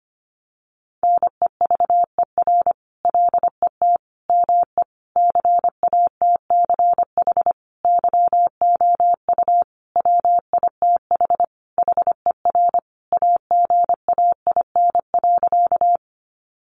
Morse code, heard as a keyed tone, about 25 wpm, 700 Hz. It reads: NEVER LET ME CATCH YOU WITH HER AGAIN.
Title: Sentences from Top 1500 Words 25wpm